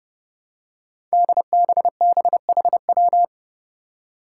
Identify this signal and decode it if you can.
Morse code, heard as a keyed tone, about 30 wpm, 700 Hz. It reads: DBBHW
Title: Five Challenging Letter-Number Combinations 30wpm